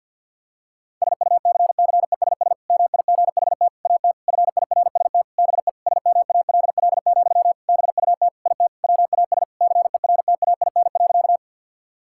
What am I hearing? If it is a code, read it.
Morse code, heard as a keyed tone, about 48 wpm, 700 Hz. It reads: SUCCESS MIGHT AT FIRST BE SMALL, BUT IT WAS CERTAIN.
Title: Sentences from Top 1000 Words - Review Part 6 - 50wpm